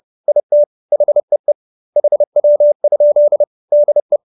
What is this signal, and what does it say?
Morse code, heard as a keyed tone, about 30 wpm, 570 Hz. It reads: IT HEE HW? DE